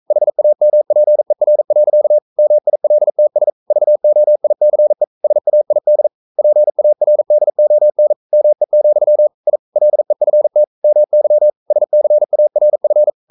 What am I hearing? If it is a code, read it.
Morse code, heard as a keyed tone, about 42 wpm, 600 Hz. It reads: HAMPER. MILTS VOICE SAID PARDON ME, I LEFT MY SCARF
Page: Romanen Never af Ken Follet på engelsk.